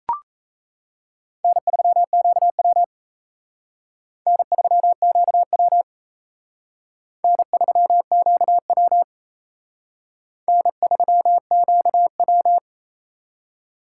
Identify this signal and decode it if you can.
Morse code, heard as a keyed tone, about 32 wpm, 700 Hz. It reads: N3QW N3QW N3QW N3QW
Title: Speed Racing - USA 1x2 Call Signs 28wpm